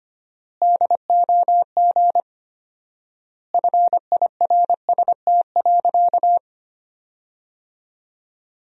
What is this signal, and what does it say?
Morse code, heard as a keyed tone, about 25 wpm, 700 Hz. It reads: DOG FIRST.
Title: 4x Word-Spacing - Sentences from Top 400 Words - 25wpm